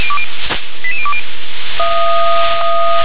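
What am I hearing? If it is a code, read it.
Five-tone selective call (ZVEI), heard as quick runs of five tones, tones of about 70 ms. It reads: 88020
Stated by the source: ZVEI five-tone call received by SDR.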